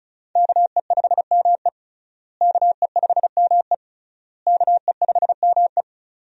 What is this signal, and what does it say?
Morse code, heard as a keyed tone, about 36 wpm, 700 Hz. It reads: KE5ME KE5ME KE5ME
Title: USA Call Signs - 5 Characters - Repeated 3x - 35wpm